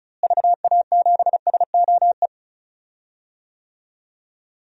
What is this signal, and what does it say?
Morse code, heard as a keyed tone, about 36 wpm, 700 Hz. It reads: VA7SOE
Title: DXCC-100 Call Signs - 6 Characters - Repeated 2x - 35wpm